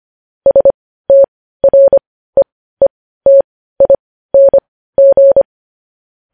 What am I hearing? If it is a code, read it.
Morse code, heard as a keyed tone, about 25 wpm, spaced as for 15 wpm, 550 Hz. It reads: STREETING
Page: news-headlines-in-morse-code-at-15-wpm